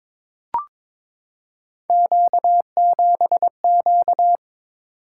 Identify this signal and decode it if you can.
Morse code, heard as a keyed tone, about 22 wpm, 700 Hz. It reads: Q7Q
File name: practice / 22wpm / Three Letter-Number Combinations - 22wpm